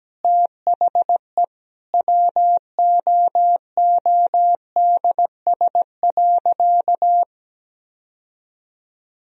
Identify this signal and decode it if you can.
Morse code, heard as a keyed tone, about 17 wpm, 700 Hz. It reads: THE WOODS.
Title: Sentences from Top 1000 Words - Review Part 2 - 17wpm